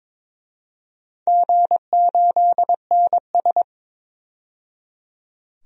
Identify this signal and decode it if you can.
Morse code, heard as a keyed tone, about 22 wpm, 700 Hz. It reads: G8NS
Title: DXCC-100 Call Signs - 4 Characters - 22wpm